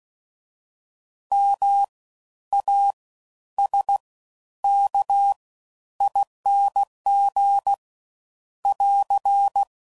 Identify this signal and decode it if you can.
Morse code, heard as a keyed tone, about 16 wpm, spaced as for 11 wpm, 800 Hz. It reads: MASKING <AR>
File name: Suffixes / Special INGs / Fast Ings